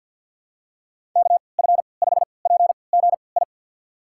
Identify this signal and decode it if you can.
Morse code, heard as a keyed tone, about 52 wpm, spaced as for 29 wpm, 700 Hz. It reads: KF5PGI
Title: USA Call Signs - 3x Character Spacing - 50wpm